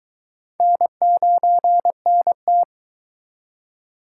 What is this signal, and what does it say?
Morse code, heard as a keyed tone, about 22 wpm, 700 Hz. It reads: N9NT